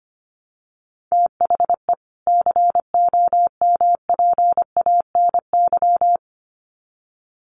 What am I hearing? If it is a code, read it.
Morse code, heard as a keyed tone, about 25 wpm, 700 Hz. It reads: THE COMPANY
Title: Sets of 2 Words 25wpm